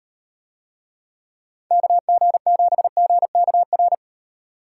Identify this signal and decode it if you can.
Morse code, heard as a keyed tone, about 38 wpm, 700 Hz. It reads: KG7GKR